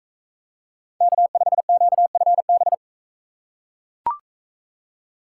Morse code, heard as a keyed tone, about 39 wpm, 700 Hz. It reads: K5QFB